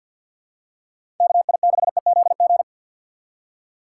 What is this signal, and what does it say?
Morse code, heard as a keyed tone, about 52 wpm, 700 Hz. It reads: KI6EZG
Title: USA Call Signs - 6 Characters - 6x Character Spacing - 50wpm